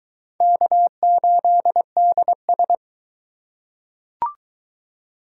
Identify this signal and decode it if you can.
Morse code, heard as a keyed tone, about 19 wpm, 700 Hz. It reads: K8DS